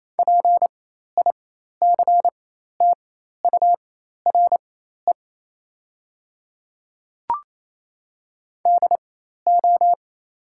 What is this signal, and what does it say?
Morse code, heard as a keyed tone, about 28 wpm, spaced as for 13 wpm, 700 Hz. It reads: PICTURE DO
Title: 4x Character Spacing - Top 200 Words - 28wpm